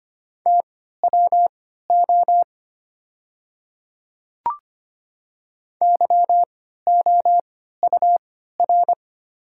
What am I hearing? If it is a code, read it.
Morse code, heard as a keyed tone, about 25 wpm, spaced as for 14 wpm, 700 Hz. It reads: TWO YOUR
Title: 3x Character Spacing - Top 200 Words - 25wpm